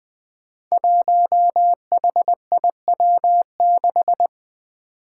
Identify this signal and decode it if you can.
Morse code, heard as a keyed tone, about 20 wpm, 700 Hz. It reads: 1HIW6
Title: Five Challenging Letter-Number Combinations 20wpm